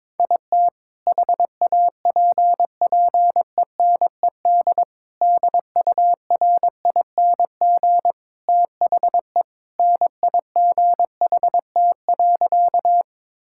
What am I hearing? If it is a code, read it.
Morse code, heard as a keyed tone, about 22 wpm, 700 Hz. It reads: IT HAPPENED DURING THE NIGHT.
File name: practice / 22wpm / Sentences with 2-Word Phrases - 22wpm